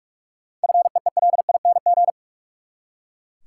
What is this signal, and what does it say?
Morse code, heard as a keyed tone, about 43 wpm, 700 Hz. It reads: FEELING
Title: Seven Letter Words - 45wpm